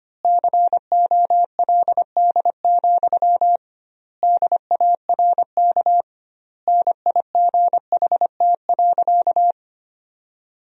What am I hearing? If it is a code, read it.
Morse code, heard as a keyed tone, about 25 wpm, 700 Hz. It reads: COLD, DARK NIGHT.
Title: 2x Word-Spacing - Sentences from Top 500 Words - 25wpm